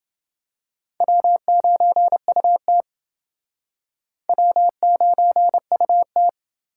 Morse code, heard as a keyed tone, about 29 wpm, 700 Hz. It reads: W9UT W9UT